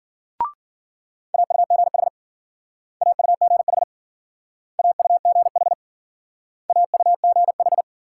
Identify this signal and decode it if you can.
Morse code, heard as a keyed tone, about 51 wpm, 700 Hz. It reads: AUGH AUGH AUGH AUGH